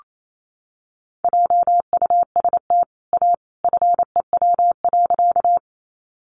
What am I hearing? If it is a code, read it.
Morse code, heard as a keyed tone, about 28 wpm, 700 Hz. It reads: JUST A FEW.
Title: Sentences from Top 300 Words 28wpm